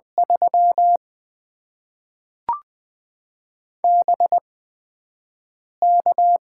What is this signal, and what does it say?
Morse code, heard as a keyed tone, about 20 wpm, spaced as for 5 wpm, 700 Hz. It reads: 3 BK
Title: Top 90 QSO Elements - 8x Character Spacing - 20wpm